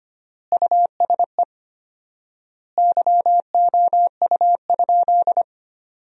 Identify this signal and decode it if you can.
Morse code, heard as a keyed tone, about 25 wpm, 700 Hz. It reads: USE YOU?
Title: 4x Word-Spacing - Sentences from Top 200 Words - 25wpm